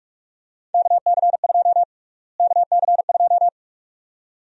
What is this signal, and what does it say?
Morse code, heard as a keyed tone, about 45 wpm, 700 Hz. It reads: KC2 KC2
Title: USA Call Sign Prefixes - Repeated 2x - 45wpm